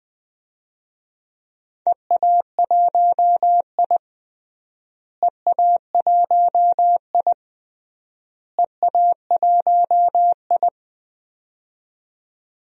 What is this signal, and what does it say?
Morse code, heard as a keyed tone, about 20 wpm, 700 Hz. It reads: EA1I EA1I EA1I
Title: DXCC-100 Call Signs - 4 Characters - Repeated 3x - 20wpm